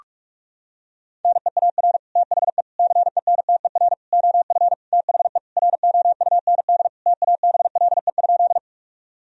Morse code, heard as a keyed tone, about 41 wpm, 700 Hz. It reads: NEAR THE CENTER OF THE ROUND TABLE?
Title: Sentences from Top 1000 Words - Review Part 2 - 45wpm